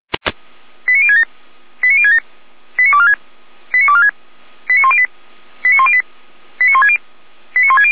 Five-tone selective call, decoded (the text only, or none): 89067 89067 89246 89246 89108 89108 89160 89160